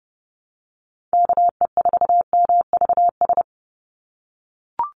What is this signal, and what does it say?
Morse code, heard as a keyed tone, about 30 wpm, 700 Hz. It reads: KE4MVS
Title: USA Call Signs 30wpm